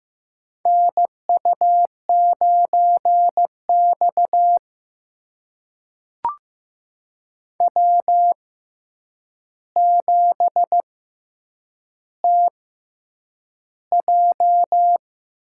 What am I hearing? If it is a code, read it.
Morse code, heard as a keyed tone, about 15 wpm, spaced as for 8 wpm, 700 Hz. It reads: NU9X W7TJ
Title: USA Call Signs - 4 Characters - 6x Character Spacing - 15wpm